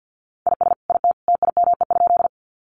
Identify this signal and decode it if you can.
Morse code, heard as a keyed tone, about 50 wpm, 700 Hz. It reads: IS IT TIME?
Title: Sentences from Top 100 Words 50wpm